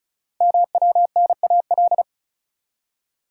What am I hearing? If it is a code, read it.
Morse code, heard as a keyed tone, about 36 wpm, 700 Hz. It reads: MWNAL
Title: Code Groups - 5 Letters - 35wpm